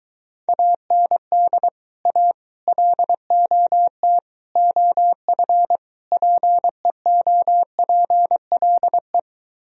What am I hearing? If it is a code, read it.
Morse code, heard as a keyed tone, about 22 wpm, 700 Hz. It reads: AND A LOT OF PEOPLE